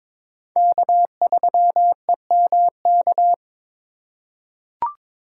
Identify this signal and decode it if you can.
Morse code, heard as a keyed tone, about 22 wpm, 700 Hz. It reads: K3EMK
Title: USA Call Signs - 5 Characters - 2x Character Spacing - 22wpm